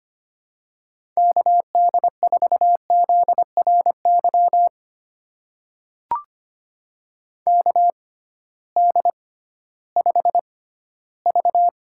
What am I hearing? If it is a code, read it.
Morse code, heard as a keyed tone, about 25 wpm, spaced as for 15 wpm, 700 Hz. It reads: KD4ZRY KD5V
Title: USA Call Signs - 6 Characters - 6x Character Spacing - 25wpm